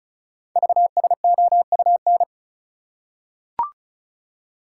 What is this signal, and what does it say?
Morse code, heard as a keyed tone, about 36 wpm, 700 Hz. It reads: VSOUN
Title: Code Groups - 5 Letters - 35wpm